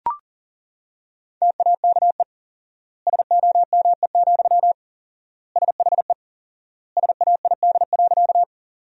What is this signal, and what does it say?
Morse code, heard as a keyed tone, about 40 wpm, 700 Hz. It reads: TAKE SOME, SHE SAID.
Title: 4x Word-Spacing - Sentences from Top 200 Words - 40wpm